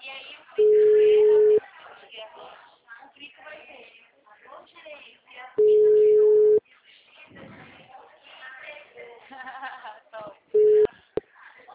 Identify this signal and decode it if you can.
Ringback tone (Europe, 425 Hz), with speech.